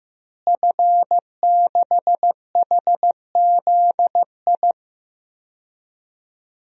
Morse code, heard as a keyed tone, about 16 wpm, 700 Hz. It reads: F6HZI
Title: DXCC-100 Call Signs - 5 Characters - Repeated 3x - 15wpm